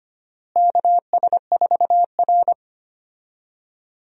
Morse code, heard as a keyed tone, about 25 wpm, 700 Hz. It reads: KS4R
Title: USA Call Signs - 4 Characters - Repeated 3x - 25wpm